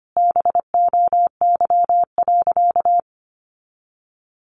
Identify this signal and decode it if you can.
Morse code, heard as a keyed tone, about 25 wpm, 700 Hz. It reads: BOY.
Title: Sentences from Top 1100 Words 25wpm